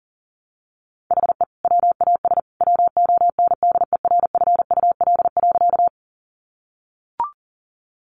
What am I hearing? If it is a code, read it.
Morse code, heard as a keyed tone, about 40 wpm, 700 Hz. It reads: HE WAS WONDERFUL.
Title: Sentences from Top 1100 Words 40wpm